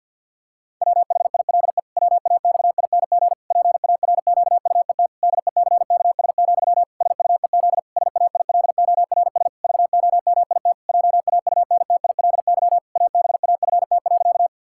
Morse code, heard as a keyed tone, about 50 wpm, 700 Hz. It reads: WHILE WAXING PARQUET DECKS, SUEZ SAILORS VOMIT JAUNTILY ABAFT.
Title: Pangram Challenge v2 - 50wpm